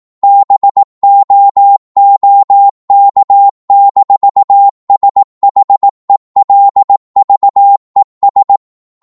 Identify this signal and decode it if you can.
Morse code, heard as a keyed tone, about 18 wpm, 800 Hz. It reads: BOOK-SHELVES
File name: Morse / psg-18wpm